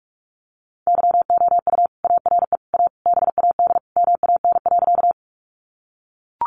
Morse code, heard as a keyed tone, about 44 wpm, 700 Hz. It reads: YOU ARE A BAD MAN.
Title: Sentences from Top 700 Words 45wpm